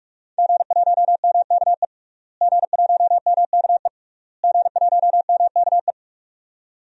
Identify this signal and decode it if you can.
Morse code, heard as a keyed tone, about 47 wpm, 700 Hz. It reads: G1MKE G1MKE G1MKE
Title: DXCC-100 Call Signs - 5 Characters - Repeated 3x - 45wpm